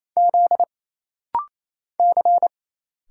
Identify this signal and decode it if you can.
Morse code, heard as a keyed tone, about 28 wpm, 700 Hz. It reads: Z C
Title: Single Letter-Number Alt - Rapid-Fire 28wpm